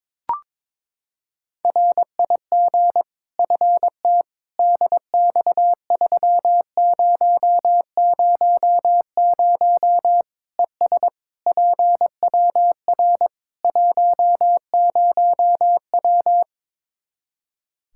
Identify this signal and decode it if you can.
Morse code, heard as a keyed tone, about 22 wpm, 700 Hz. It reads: RIG FT DX3000 ES PWR 10W
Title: Practice QSO - Exchange 2 - Info - 22wpm